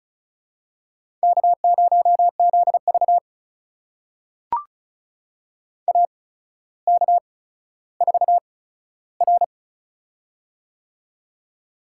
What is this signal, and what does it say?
Morse code, heard as a keyed tone, about 36 wpm, spaced as for 15 wpm, 700 Hz. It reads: K0ZV AK4R
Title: USA Call Signs - 4 Characters - 8x Character Spacing - 35wpm